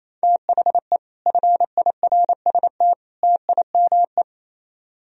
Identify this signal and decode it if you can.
Morse code, heard as a keyed tone, about 37 wpm, 700 Hz. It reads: THE FIRST TIME